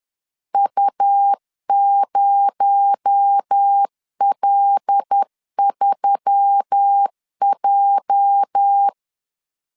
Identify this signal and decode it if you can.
Morse code, heard as a keyed tone, about 10 wpm, 800 Hz. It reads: U0L3J
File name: LEZIONI / 8 YJ45 / 8 YJ 10wpm 2651